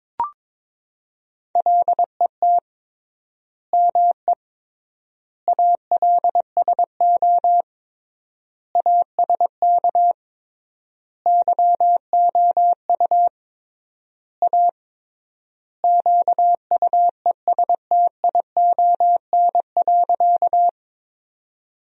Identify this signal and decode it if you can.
Morse code, heard as a keyed tone, about 21 wpm, 700 Hz. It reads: LET ME ALSO ASK YOU A QUESTION.
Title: 3x Word-Spacing - Sentences from Top 400 Words - 22wpm